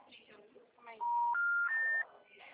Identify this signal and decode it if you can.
Special information tone, with speech.